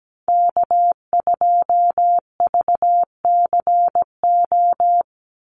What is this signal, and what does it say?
Morse code, heard as a keyed tone, about 17 wpm, 700 Hz. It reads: K2VCO
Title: Call Signs - 17wpm